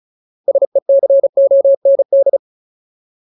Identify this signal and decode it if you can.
Morse code, heard as a keyed tone, about 34 wpm, 550 Hz. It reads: SECOND
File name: Ultimate Word List / 35WPM Ultimate